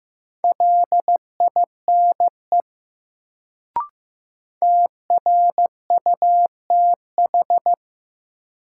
Morse code, heard as a keyed tone, about 15 wpm, 700 Hz. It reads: LINE TRUTH
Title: Sets of 5 Words - New Word Drill - 15wpm